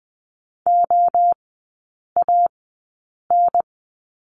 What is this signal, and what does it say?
Morse code, heard as a keyed tone, about 20 wpm, 700 Hz. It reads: O A N
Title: 023 Lesson 5e - Letter o - Three Characters 20wpm